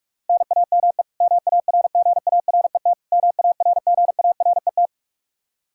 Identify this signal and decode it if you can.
Morse code, heard as a keyed tone, about 46 wpm, 700 Hz. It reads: NAME MARGARET MARGARET
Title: Practice QSO - Exchange 1 - Beginning - 45wpm